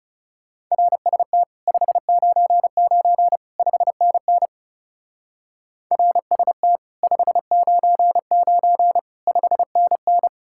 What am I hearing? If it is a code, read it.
Morse code, heard as a keyed tone, about 37 wpm, 700 Hz. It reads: RST 599 5NN RST 599 5NN
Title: Speed-Racing Common QSO Phrases - 30wpm